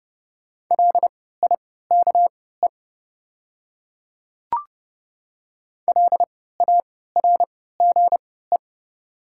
Morse code, heard as a keyed tone, about 30 wpm, spaced as for 17 wpm, 690 Hz. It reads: LIKE LARGE
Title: 3x Character Spacing - Top 200 Words - 30wpm